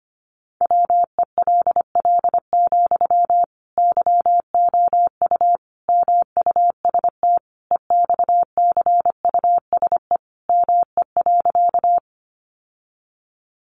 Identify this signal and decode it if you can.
Morse code, heard as a keyed tone, about 25 wpm, 700 Hz. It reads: WELL, YOU MUST EXCUSE ME.
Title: Sentences from Top 700 Words 25wpm